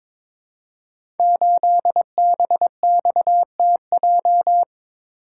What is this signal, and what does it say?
Morse code, heard as a keyed tone, about 22 wpm, 700 Hz. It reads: 8BXTJ
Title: Code Groups - 5 Letter:Numbers - 22wpm